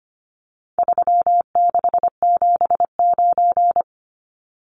Morse code, heard as a keyed tone, about 25 wpm, 700 Hz. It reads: 3679
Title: Numbers - 4-Digit 25wpm